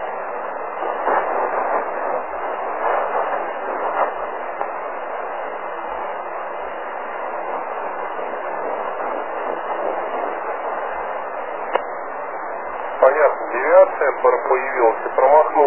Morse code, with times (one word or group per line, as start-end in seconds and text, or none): none